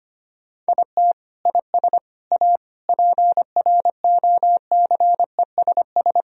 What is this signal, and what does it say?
Morse code, heard as a keyed tone, about 25 wpm, 700 Hz. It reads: IT IS A PROCESS
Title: Sentences from Top 1000 Words - Review Part 4 - 25wpm